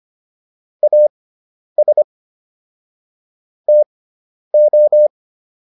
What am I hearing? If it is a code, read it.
Morse code, heard as a keyed tone, about 25 wpm, spaced as for 10 wpm, 600 Hz. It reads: AS TO